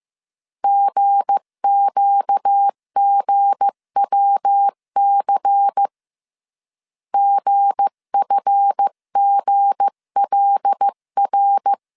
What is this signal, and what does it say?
Morse code, heard as a keyed tone, about 15 wpm, 800 Hz. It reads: GQGWC GFGLR
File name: CW / DET / 1 9 1500C S B 14wpm 3302